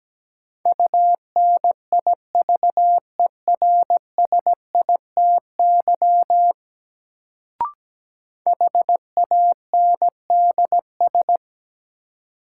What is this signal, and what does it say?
Morse code, heard as a keyed tone, about 17 wpm, 700 Hz. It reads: UNIVERSITY HANDS
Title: Sets of 4 Words - New Word Drill - 17wpm